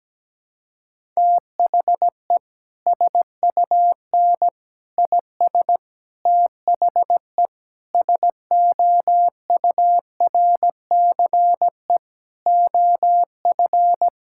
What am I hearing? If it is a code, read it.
Morse code, heard as a keyed tone, about 17 wpm, 700 Hz. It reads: THE SUN IS THE SOURCE OF
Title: Sentences from Top 1000 Words - Review Part 5 - 17wpm